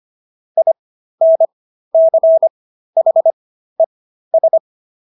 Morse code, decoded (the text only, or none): INCHES